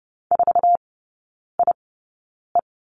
Morse code, heard as a keyed tone, about 30 wpm, spaced as for 9 wpm, 700 Hz. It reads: 4IE
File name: Lesson 17e - Letter u - Three Characters 30wpm